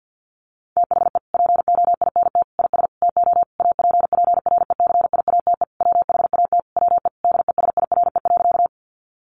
Morse code, heard as a keyed tone, about 50 wpm, 700 Hz. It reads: THE POINT IS TO APPRECIATE WHAT WE DESIRE.
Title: Sentences from Top 1300 Words 50wpm